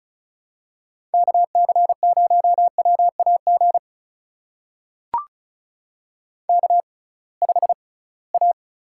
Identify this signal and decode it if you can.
Morse code, heard as a keyed tone, about 36 wpm, spaced as for 23 wpm, 700 Hz. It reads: KC0WAG K5A
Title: USA Call Signs - 6x Character Spacing - 35wpm